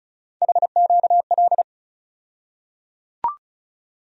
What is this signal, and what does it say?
Morse code, heard as a keyed tone, about 36 wpm, 700 Hz. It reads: HQL
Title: USA Call Sign Postfixes - 3 Letters 35wpm